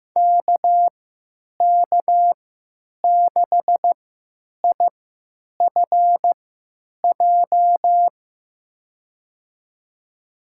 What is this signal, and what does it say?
Morse code, heard as a keyed tone, about 15 wpm, spaced as for 9 wpm, 700 Hz. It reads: KK6IFJ
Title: USA Call Signs - 6 Characters - 3x Character Spacing - 15wpm